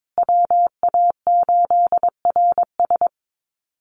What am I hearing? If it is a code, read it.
Morse code, heard as a keyed tone, about 22 wpm, 690 Hz. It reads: WA8RS
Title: USA Call Signs Repeated 3x 22wpm